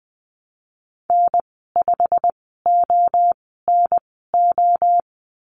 Morse code, heard as a keyed tone, about 20 wpm, spaced as for 15 wpm, 700 Hz. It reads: N5ONO
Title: 088 Lesson 15g - Number 5 - Call Signs 20wpm